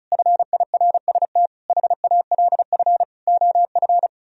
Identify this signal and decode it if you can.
Morse code, heard as a keyed tone, about 43 wpm, 700 Hz. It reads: FIRST HALF OF
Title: Speed-Racing - Sets of 5 Words 35wpm